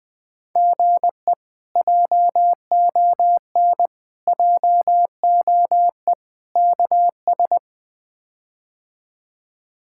Morse code, heard as a keyed tone, about 20 wpm, 700 Hz. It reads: GE JON JOE KS
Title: K1USN SST QSO Practice - 20wpm